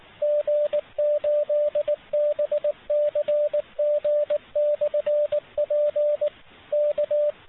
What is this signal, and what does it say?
Morse code, heard as a keyed tone, about 24 wpm, 590 Hz. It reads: G8BCG/P K
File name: fp-n6ra-cwqso